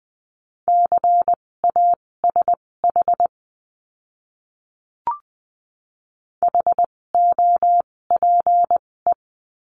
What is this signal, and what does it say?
Morse code, heard as a keyed tone, about 20 wpm, spaced as for 16 wpm, 700 Hz. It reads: CASH HOPE
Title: 256 Lesson 42a - Words 12-20wpm